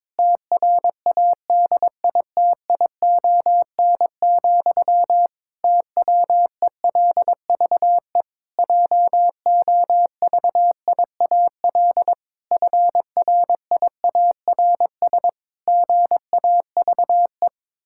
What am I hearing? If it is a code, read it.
Morse code, heard as a keyed tone, about 22 wpm, 700 Hz. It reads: TRADITION, TWELVE JOVIAL FRIARS GAVE
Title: Pangram Challenge v2 - 22wpm